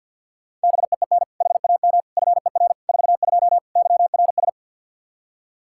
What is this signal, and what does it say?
Morse code, heard as a keyed tone, about 46 wpm, 700 Hz. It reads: BEEN HAM FER 42 YRS
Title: Practice QSO - Exchange 3 - More Info - 50wpm